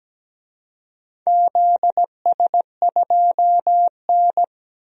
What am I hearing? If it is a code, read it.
Morse code, heard as a keyed tone, about 17 wpm, 700 Hz. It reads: ZS2N
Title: DXCC-100 Call Signs - 4 Characters - 17wpm